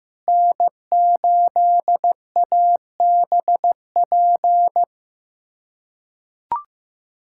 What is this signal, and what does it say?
Morse code, heard as a keyed tone, about 15 wpm, 700 Hz. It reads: N8ABP
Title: USA Callsign DE Callsign - 5 Characters - 15wpm